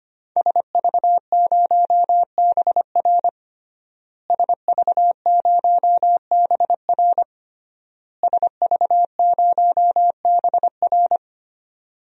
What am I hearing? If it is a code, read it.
Morse code, heard as a keyed tone, about 25 wpm, 700 Hz. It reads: SV0BR SV0BR SV0BR
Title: DXCC-100 Call Signs - 5 Characters - Repeated 3x - 25wpm